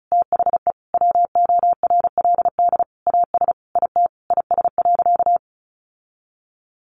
Morse code, heard as a keyed tone, about 34 wpm, 700 Hz. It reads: THE WORLD AS IT IS.
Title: Sentences from Top 1200 Words 35wpm